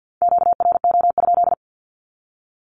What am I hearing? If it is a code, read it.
Morse code, heard as a keyed tone, about 50 wpm, 700 Hz. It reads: QRO?
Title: Q-Codes 50wpm